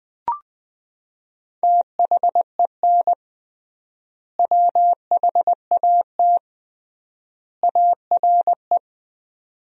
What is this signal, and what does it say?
Morse code, heard as a keyed tone, about 20 wpm, 700 Hz. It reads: THEN WHAT ARE
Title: 3x Word-Spacing - Sentences from Top 100 Words - 20wpm